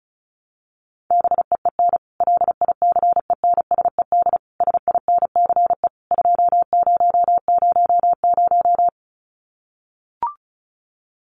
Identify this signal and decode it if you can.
Morse code, heard as a keyed tone, about 34 wpm, 700 Hz. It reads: BEEN LICENSED SINCE 2000
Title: Common QSO Phrases 35wpm